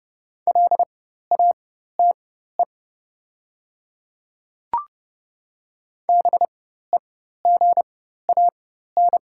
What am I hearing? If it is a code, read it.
Morse code, heard as a keyed tone, about 30 wpm, spaced as for 14 wpm, 700 Hz. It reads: LATE BEGAN
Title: 4x Character Spacing - Top 300 Words - 30wpm